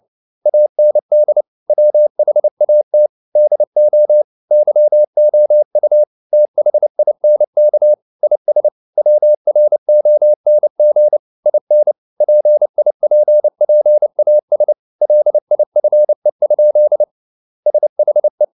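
Morse code, heard as a keyed tone, about 29 wpm, 600 Hz. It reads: AND WHAT DO YOU THINK IS WRONG IN PIPPAS LIFE? SHE
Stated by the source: Morse code at 29 wpm, sending Ken Follett's novel Never in English, from CW med Gnister.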